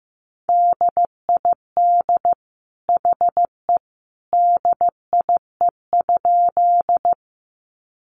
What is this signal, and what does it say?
Morse code, heard as a keyed tone, about 14 wpm, 700 Hz. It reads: DID HE DIE?
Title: Sentences from Top 1300 Words 15wpm